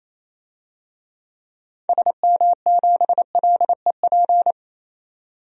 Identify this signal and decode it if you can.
Morse code, heard as a keyed tone, about 27 wpm, 700 Hz. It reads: SM7LEP
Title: DXCC-100 Call Signs - 6 Characters - 28wpm